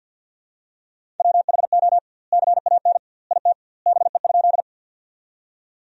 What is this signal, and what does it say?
Morse code, heard as a keyed tone, about 50 wpm, 700 Hz. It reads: WHO CAN IT BE?
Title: 2x Word-Spacing - Sentences from Top 100 Words - 50wpm